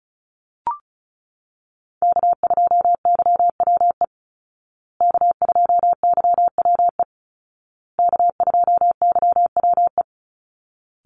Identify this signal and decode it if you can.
Morse code, heard as a keyed tone, about 34 wpm, 700 Hz. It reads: K2YWE K2YWE K2YWE
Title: Call Signs Repeated 2x - 35wpm